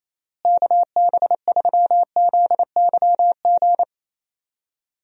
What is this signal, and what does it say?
Morse code, heard as a keyed tone, about 28 wpm, 700 Hz. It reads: KB3ZYG
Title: USA Call Signs - 6 Characters - Repeated 3x - 28wpm